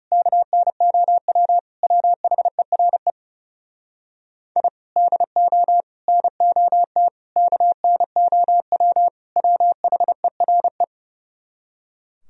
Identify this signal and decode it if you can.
Morse code, heard as a keyed tone, about 39 wpm, 700 Hz. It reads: KNOW WHERE I DO NOT KNOW WHERE